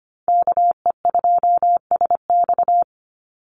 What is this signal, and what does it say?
Morse code, heard as a keyed tone, about 25 wpm, 700 Hz. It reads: KE2SX
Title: Call Signs - 25wpm